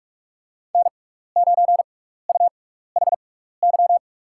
Morse code, heard as a keyed tone, about 46 wpm, spaced as for 15 wpm, 700 Hz. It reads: N9UHY
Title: USA Call Signs - 5 Characters - 6x Character Spacing - 45wpm